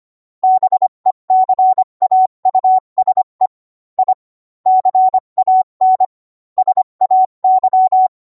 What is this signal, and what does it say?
Morse code, heard as a keyed tone, about 25 wpm, 750 Hz. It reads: BECAUSE I CAN SAY